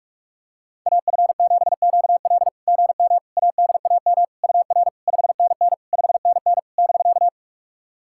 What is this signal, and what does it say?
Morse code, heard as a keyed tone, about 44 wpm, 690 Hz. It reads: AF7QL GM ADAM UR 5NN 5NN <BK>
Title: SOTA QSO Practice 45wpm